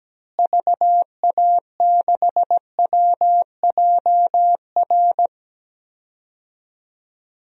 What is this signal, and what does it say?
Morse code, heard as a keyed tone, about 17 wpm, 700 Hz. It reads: VA6WJR
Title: DXCC-100 Call Signs - 6 Characters - Repeated 2x - 17wpm